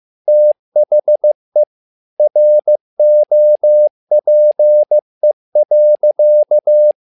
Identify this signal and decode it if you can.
Morse code, heard as a keyed tone, about 15 wpm, 600 Hz. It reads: THE ROPE.
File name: learning center / tower trouble / tower trouble 15wpm